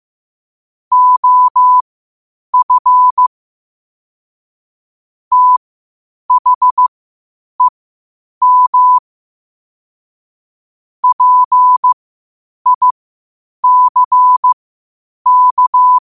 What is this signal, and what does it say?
Morse code, heard as a keyed tone, about 15 wpm, spaced as for 8 wpm, 1000 Hz. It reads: OF THEM PICK
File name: qotdfiles / 8wpm